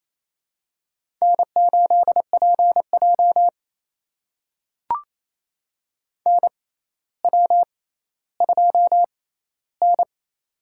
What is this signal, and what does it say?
Morse code, heard as a keyed tone, about 28 wpm, spaced as for 14 wpm, 700 Hz. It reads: N8PJ NW2N
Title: USA Call Signs - 4 Characters - 6x Character Spacing - 28wpm